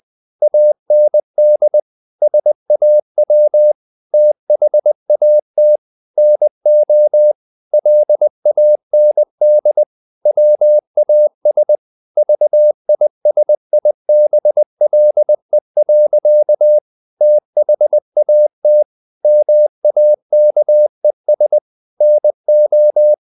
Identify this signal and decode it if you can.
Morse code, heard as a keyed tone, about 20 wpm, 600 Hz. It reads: AND SAW THAT NO LAND WAS VISIBLE. THAT MAKES NO
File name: the old man and the sea / the old man and the sea part 1 20wpm